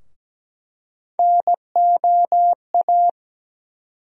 Morse code, heard as a keyed tone, about 17 wpm, 700 Hz. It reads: NOA
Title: USA Call Sign Postfixes - 3 Letters 17wpm